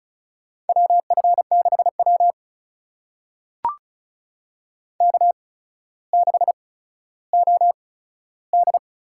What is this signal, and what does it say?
Morse code, heard as a keyed tone, about 36 wpm, spaced as for 15 wpm, 700 Hz. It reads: WF6W K6OD
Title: USA Call Signs - 4 Characters - 8x Character Spacing - 35wpm